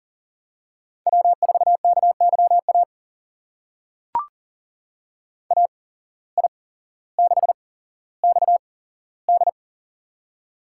Morse code, heard as a keyed tone, about 39 wpm, spaced as for 17 wpm, 700 Hz. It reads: W4KYA AI6XD
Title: USA Call Signs - 5 Characters - 8x Character Spacing - 40wpm